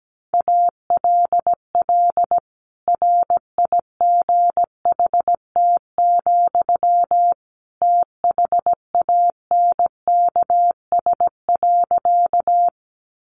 Morse code, heard as a keyed tone, about 17 wpm, 700 Hz. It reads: ALL RIGHT, THANKS.
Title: Sentences from Top 600 Words 17wpm